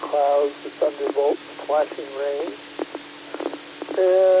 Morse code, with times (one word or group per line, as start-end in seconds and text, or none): none